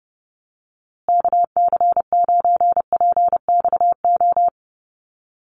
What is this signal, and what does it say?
Morse code, heard as a keyed tone, about 30 wpm, 700 Hz. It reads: KC9PXO
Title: USA Call Signs Repeated 3x 30wpm